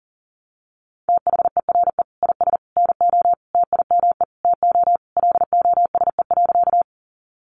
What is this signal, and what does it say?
Morse code, heard as a keyed tone, about 40 wpm, 700 Hz. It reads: THERE IS NO TIME TO LOSE.
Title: Sentences from Top 1100 Words 40wpm